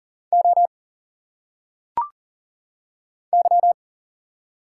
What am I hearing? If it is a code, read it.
Morse code, heard as a keyed tone, about 40 wpm, 700 Hz. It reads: O Y
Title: Single Letters - Phonetic Pronunciation 40wpm